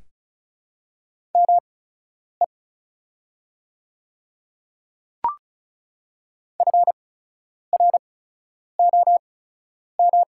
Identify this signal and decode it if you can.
Morse code, heard as a keyed tone, about 34 wpm, spaced as for 10 wpm, 690 Hz. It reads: ME FROM
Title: 8x Character Spacing - Top 200 Words - 35wpm